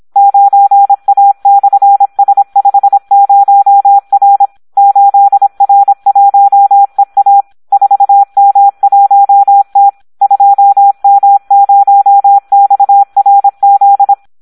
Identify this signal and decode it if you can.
Morse code, heard as a keyed tone, about 26 wpm, 800 Hz. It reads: 9A/S50R 8R1EA 4M1T 2M0XRZ
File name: Audio practica CW / SD 1 26